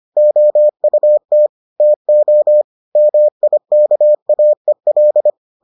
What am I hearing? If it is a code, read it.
Morse code, heard as a keyed tone, about 25 wpm, 600 Hz. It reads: OUT TO MIKAEL